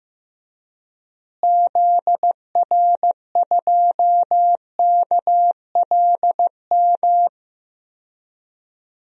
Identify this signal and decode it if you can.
Morse code, heard as a keyed tone, about 15 wpm, 700 Hz. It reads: ZR2KLM
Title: DXCC-100 Call Signs - 6 Characters - Repeated 2x - 15wpm